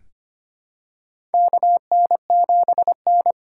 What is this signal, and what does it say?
Morse code, heard as a keyed tone, about 25 wpm, 700 Hz. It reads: KN7N
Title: USA Call Signs - 4 Characters - 2x Character Spacing - 25wpm